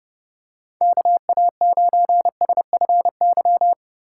Morse code, heard as a keyed tone, about 29 wpm, 700 Hz. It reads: KA9SFY